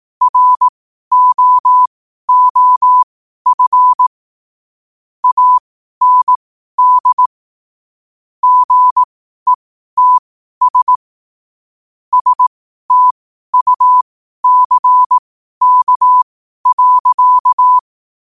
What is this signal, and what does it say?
Morse code, heard as a keyed tone, about 18 wpm, spaced as for 12 wpm, 1000 Hz. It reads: ROOF AND GETS STUCK.
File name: qotdfiles / 12wpm